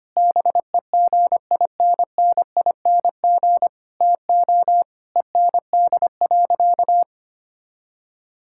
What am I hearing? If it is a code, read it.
Morse code, heard as a keyed tone, about 25 wpm, 700 Hz. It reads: BEGINNING TO END.
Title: Sentences from Top 1000 Words - Review Part 5 - 25wpm